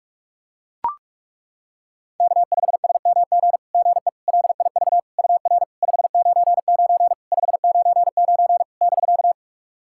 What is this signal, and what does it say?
Morse code, heard as a keyed tone, about 41 wpm, 690 Hz. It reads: K5SMG GE LIV UR 599 599 <BK>
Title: SOTA QSO Practice 45wpm